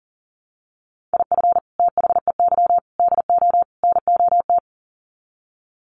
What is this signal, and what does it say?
Morse code, heard as a keyed tone, about 40 wpm, 700 Hz. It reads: IF THEY DO NOT
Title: Sets of 4 Words 40wpm